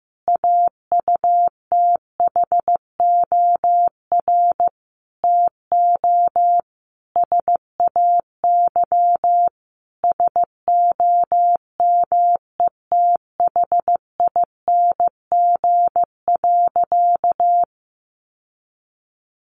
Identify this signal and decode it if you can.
Morse code, heard as a keyed tone, about 15 wpm, 700 Hz. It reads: AUTHOR TO SAY SOMETHING.
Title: Sentences from Top 1200 Words 15wpm